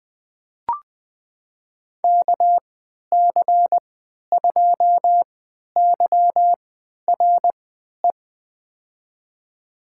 Morse code, heard as a keyed tone, about 20 wpm, spaced as for 11 wpm, 700 Hz. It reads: KC2YRE
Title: USA Call Signs - 6 Characters - 3x Character Spacing - 20wpm